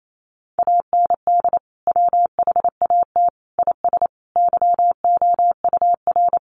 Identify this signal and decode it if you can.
Morse code, heard as a keyed tone, about 28 wpm, 700 Hz. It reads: AND WHAT IS YOUR
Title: Sentences from Top 1200 Words 28wpm